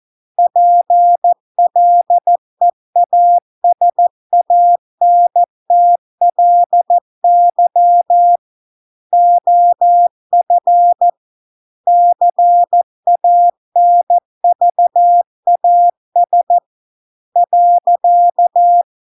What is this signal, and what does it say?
Morse code, heard as a keyed tone, about 14 wpm, 700 Hz. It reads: PLEASANTLY OF CANVAS .